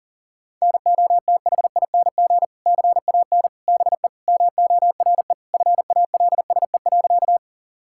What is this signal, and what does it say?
Morse code, heard as a keyed tone, about 40 wpm, 700 Hz. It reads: NOTHING CAN BE MORE FALSE.
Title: Sentences from Top 1600 Words 40wpm